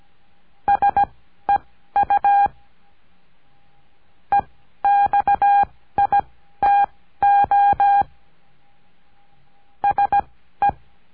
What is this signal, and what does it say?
Morse code, heard as a keyed tone, about 17 wpm, spaced as for 12 wpm, 810 Hz. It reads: SEU EXITO SE
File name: cursos / CURSO DE TELEGRAFIA / Telegrafia - Trilha 17